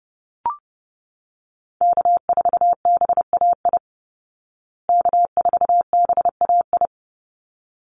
Morse code, heard as a keyed tone, about 30 wpm, 700 Hz. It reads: K4BAI K4BAI
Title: Call Signs Repeated 3x - 30wpm